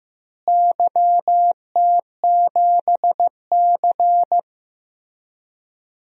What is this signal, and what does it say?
Morse code, heard as a keyed tone, about 15 wpm, 700 Hz. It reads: YT7C
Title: DXCC-100 Call Signs - 4 Characters - Repeated 3x - 15wpm